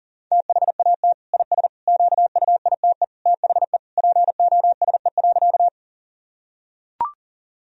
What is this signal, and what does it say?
Morse code, heard as a keyed tone, about 40 wpm, 700 Hz. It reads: THAT IS QUITE THE POSE.
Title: Sentences from Top 1000 Words - Review Part 2 - 40wpm